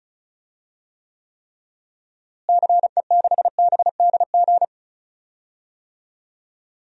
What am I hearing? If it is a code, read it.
Morse code, heard as a keyed tone, about 34 wpm, 700 Hz. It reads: CE6BDG
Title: DXCC-100 Call Signs - 6 Characters - Repeated 3x - 35wpm